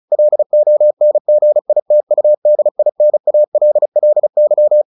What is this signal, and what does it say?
Morse code, heard as a keyed tone, about 34 wpm, 600 Hz. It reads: LONGITUDINALLY
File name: the old man and the sea / the old man and the sea 1 35wpm